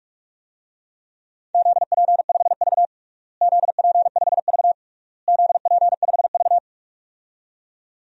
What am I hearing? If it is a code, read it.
Morse code, heard as a keyed tone, about 46 wpm, 700 Hz. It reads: ZP5V ZP5V ZP5V
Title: DXCC-100 Call Signs - 4 Characters - Repeated 3x - 45wpm